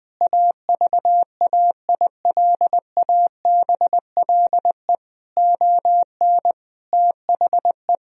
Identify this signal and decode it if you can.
Morse code, heard as a keyed tone, about 20 wpm, 700 Hz. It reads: AVAILABLE ON THE
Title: Sentences from Top 1000 Words - Review Part 3 - 20wpm